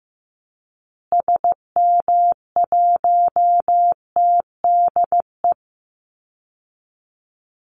Morse code, heard as a keyed tone, about 15 wpm, 700 Hz. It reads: SM1TDE
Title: Call Signs Repeated 2x - 15wpm